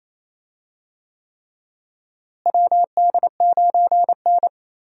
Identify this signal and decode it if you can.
Morse code, heard as a keyed tone, about 28 wpm, 700 Hz. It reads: WD9N
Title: USA Call Signs - 4 Characters - Repeated 3x - 28wpm